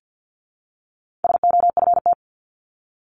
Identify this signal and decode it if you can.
Morse code, heard as a keyed tone, about 50 wpm, 700 Hz. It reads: SOFT
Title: Top 1251-1500 Words 50wpm